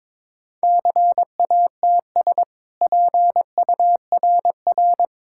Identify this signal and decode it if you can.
Morse code, heard as a keyed tone, about 22 wpm, 690 Hz. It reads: CATS PURR
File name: practice / 22wpm / Easy Sentences with Hints - 22wpm